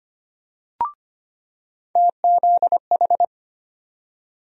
Morse code, heard as a keyed tone, about 25 wpm, 700 Hz. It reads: TZH
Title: USA Call Sign Postfixes - 3 Letters 25wpm